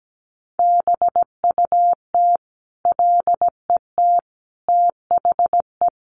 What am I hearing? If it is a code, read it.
Morse code, heard as a keyed tone, about 17 wpm, 700 Hz. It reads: BUT LET THE
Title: Sentences from Top 1300 Words 17wpm